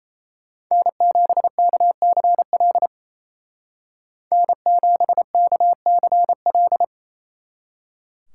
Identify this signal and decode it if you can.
Morse code, heard as a keyed tone, about 33 wpm, 700 Hz. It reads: N7KCL N7KCL
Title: Speed Racing - USA 1x3 Call Signs 28wpm